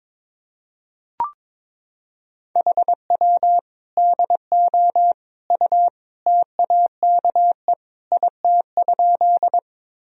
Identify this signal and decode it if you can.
Morse code, heard as a keyed tone, about 21 wpm, 700 Hz. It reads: HW DO U TAKE IT?
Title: Sentences from Top 200 Words and Abbreviations 22wpm